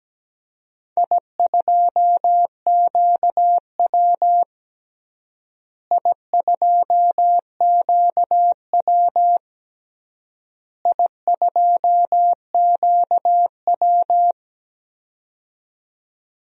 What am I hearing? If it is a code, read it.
Morse code, heard as a keyed tone, about 17 wpm, 700 Hz. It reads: I2QW I2QW I2QW
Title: DXCC-100 Call Signs - 4 Characters - Repeated 3x - 17wpm